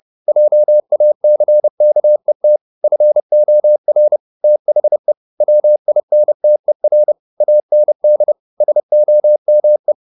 Morse code, heard as a keyed tone, about 30 wpm, 600 Hz. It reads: JACKET FOR THE WINTER AND SOME
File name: the old man and the sea / the old man and the sea 1 30wpm